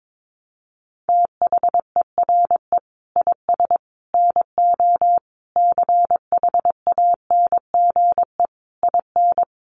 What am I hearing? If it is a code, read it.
Morse code, heard as a keyed tone, about 22 wpm, 700 Hz. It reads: THERE IS NO CHANGE IN
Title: Sentences from Top 1300 Words 22wpm